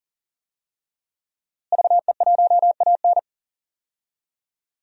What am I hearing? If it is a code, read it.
Morse code, heard as a keyed tone, about 40 wpm, 700 Hz. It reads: VE1AN
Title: DXCC-100 Call Signs - 5 Characters - Repeated 2x - 40wpm